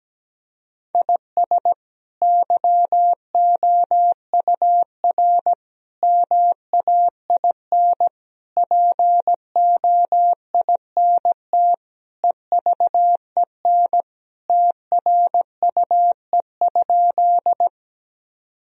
Morse code, heard as a keyed tone, about 17 wpm, 700 Hz. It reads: IS YOUR MAIN POINT EVEN TRUE?
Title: Sentences from Top 1000 Words - Review Part 2 - 17wpm